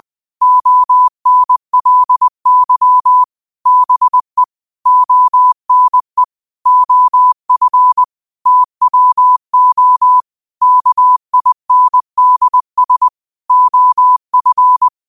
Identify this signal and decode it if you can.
Morse code, heard as a keyed tone, about 20 wpm, 1000 Hz. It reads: ONLY BE ONE OF TWO KINDS OF
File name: qotdfiles / 20wpm